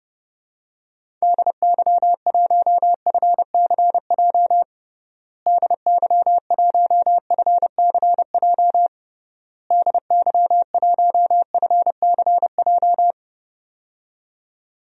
Morse code, heard as a keyed tone, about 32 wpm, 700 Hz. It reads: DY1FCJ DY1FCJ DY1FCJ
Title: DXCC-100 Call Signs - 6 Characters - Repeated 3x - 30wpm